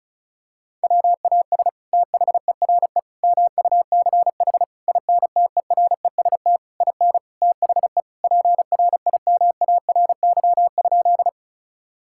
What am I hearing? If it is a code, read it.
Morse code, heard as a keyed tone, about 34 wpm, 700 Hz. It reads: WAS THERE MUCH INTEREST IN THE PRIMARY?
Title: Sentences from Top 1000 Words - Review Part 5 - 35wpm